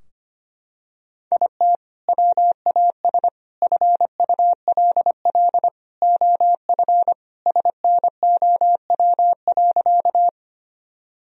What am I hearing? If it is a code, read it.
Morse code, heard as a keyed tone, about 25 wpm, 700 Hz. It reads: IT WAS FULL OF SNOW.
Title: Sentences from Top 1000 Words - Review Part 3 - 25wpm